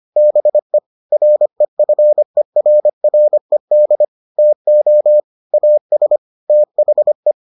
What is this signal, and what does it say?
Morse code, heard as a keyed tone, about 25 wpm, 600 Hz. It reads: BE REFERRED TO AS THE